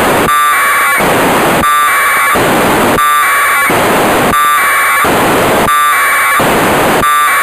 Packet radio (AX.25, as in APRS).
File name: Negai 1200bd packet received by JA6PL provided by JA0CAW